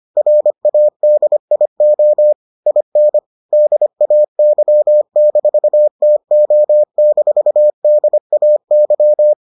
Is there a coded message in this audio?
Yes: Morse code from 0.2 to 9.3 s, heard as a keyed tone, about 25 wpm, 600 Hz. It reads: RADIO IN DAY-TO-DAY